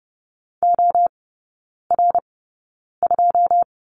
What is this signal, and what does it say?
Morse code, heard as a keyed tone, about 30 wpm, spaced as for 9 wpm, 700 Hz. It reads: OR2
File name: Lesson 17e - Letter u - Three Characters 30wpm